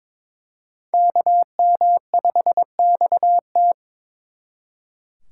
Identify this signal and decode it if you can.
Morse code, heard as a keyed tone, about 26 wpm, 700 Hz. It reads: KM5XT
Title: Speed Racing - USA 2x2 Call Signs 22wpm